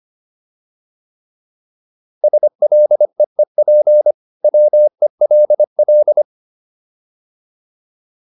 Morse code, heard as a keyed tone, about 25 wpm, 600 Hz. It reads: SLEEP WELL
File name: PR403 25